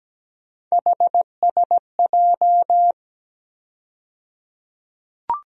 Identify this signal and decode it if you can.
Morse code, heard as a keyed tone, about 17 wpm, 700 Hz. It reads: HSJ
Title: USA Call Sign Postfixes - 3 Letters 17wpm